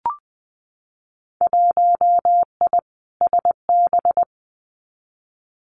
Morse code, heard as a keyed tone, about 20 wpm, 700 Hz. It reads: 1I SB
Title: Winter Field Day Exchanges Encore 20wpm